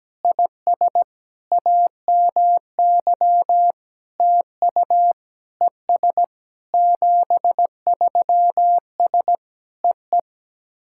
Morse code, heard as a keyed tone, about 17 wpm, 700 Hz. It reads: IS AMY TU ES 73S EE
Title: SOTA QSO Practice 17wpm